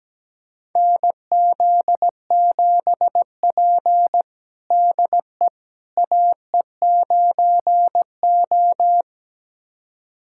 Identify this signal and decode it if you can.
Morse code, heard as a keyed tone, about 17 wpm, 700 Hz. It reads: NZ7P DE AE9O
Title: USA Callsign DE Callsign - 4 Characters - 17wpm